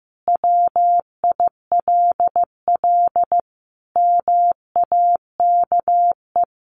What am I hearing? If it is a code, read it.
Morse code, heard as a keyed tone, about 15 wpm, 700 Hz. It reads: WILL MAKE
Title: Sentences from Top 800 Words 15wpm